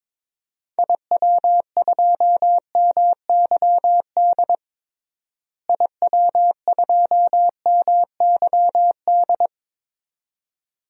Morse code, heard as a keyed tone, about 22 wpm, 700 Hz. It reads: IW2MYD IW2MYD
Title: DXCC-100 Call Signs - 6 Characters - Repeated 2x - 22wpm